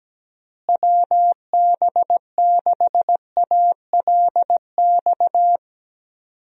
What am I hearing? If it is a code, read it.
Morse code, heard as a keyed tone, about 17 wpm, 700 Hz. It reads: WB6ALX
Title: USA Call Signs - 6 Characters - Repeated 3x - 17wpm